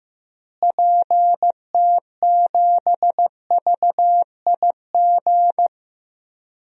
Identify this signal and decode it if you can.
Morse code, heard as a keyed tone, about 15 wpm, 700 Hz. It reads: PT7VIG
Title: DXCC-100 Call Signs - 6 Characters - Repeated 3x - 15wpm